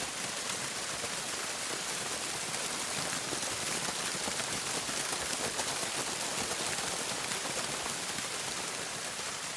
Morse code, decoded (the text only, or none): none